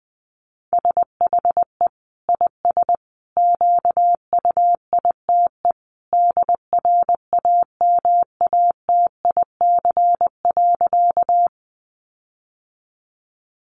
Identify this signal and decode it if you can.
Morse code, heard as a keyed tone, about 20 wpm, 700 Hz. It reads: SHE IS QUITE DRAMATIC.
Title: Sentences from Top 1300 Words 20wpm